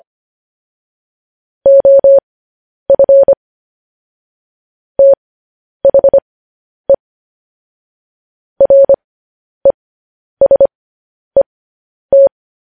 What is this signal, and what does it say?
Morse code, heard as a keyed tone, about 25 wpm, spaced as for 10 wpm, 550 Hz. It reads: OF THE RESET
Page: news headlines in morse code at 10 wpm